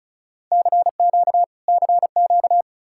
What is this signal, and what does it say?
Morse code, heard as a keyed tone, about 46 wpm, spaced as for 41 wpm, 700 Hz. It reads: CQ CQ
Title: Speed-Racing Common QSO Phrases Encore - 35wpm